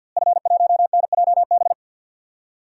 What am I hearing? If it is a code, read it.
Morse code, heard as a keyed tone, about 49 wpm, 700 Hz. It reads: F1NJB
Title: DXCC-100 Call Signs - 5 Characters - Repeated 2x - 50wpm